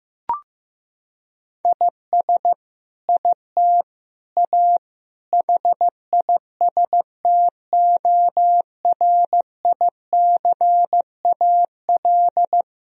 Morse code, heard as a keyed tone, about 16 wpm, 700 Hz. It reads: IS IT A HISTORICAL
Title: Sentences from Top 1000 Words - Review Part 4 - 15wpm